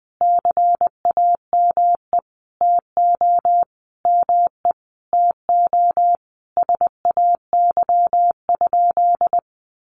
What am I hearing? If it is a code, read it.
Morse code, heard as a keyed tone, about 20 wpm, 700 Hz. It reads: CAME TO ME TO SAY?
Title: Sentences from Top 200 Words 20wpm